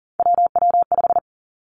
Morse code, heard as a keyed tone, about 40 wpm, 700 Hz. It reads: WW5
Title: Three Challenging Letter-Number Combinations 40wpm